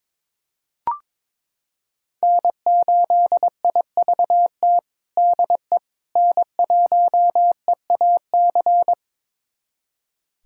Morse code, heard as a keyed tone, about 22 wpm, 700 Hz. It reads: N8IVT DE N1EAC
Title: USA Callsign DE Callsign - 5 Characters - 22wpm